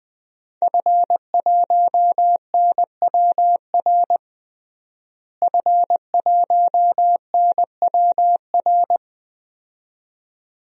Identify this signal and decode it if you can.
Morse code, heard as a keyed tone, about 21 wpm, 700 Hz. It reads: F1NWR F1NWR
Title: DXCC-100 Call Signs - 5 Characters - Repeated 2x - 20wpm